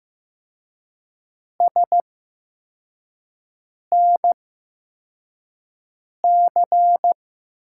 Morse code, heard as a keyed tone, about 15 wpm, spaced as for 4 wpm, 700 Hz. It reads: SNC
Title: USA Call Signs - 8x Character Spacing - 15wpm